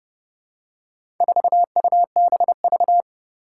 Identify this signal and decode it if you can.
Morse code, heard as a keyed tone, about 31 wpm, 700 Hz. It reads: 4UBV
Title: Four Challenging Letter-Number Combinations 30wpm